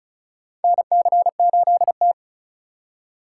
Morse code, heard as a keyed tone, about 36 wpm, 700 Hz. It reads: NC8T
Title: USA Call Signs - 4 Characters - 2x Character Spacing - 35wpm